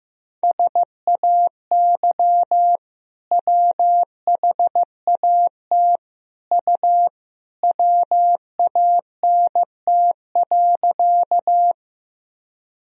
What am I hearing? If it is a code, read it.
Morse code, heard as a keyed tone, about 15 wpm, 700 Hz. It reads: SAY WHAT U WANT.
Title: Sentences from Top 200 Words and Abbreviations 15wpm